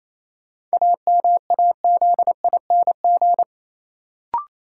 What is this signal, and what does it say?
Morse code, heard as a keyed tone, about 28 wpm, 700 Hz. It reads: AMAZING
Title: Sentences from Top 1500 Words - New Word Drill - 28wpm